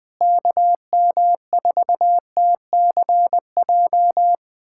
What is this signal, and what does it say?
Morse code, heard as a keyed tone, about 20 wpm, 700 Hz. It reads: KM4TCJ
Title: USA Callsign DE Callsign - 6 Characters - 20wpm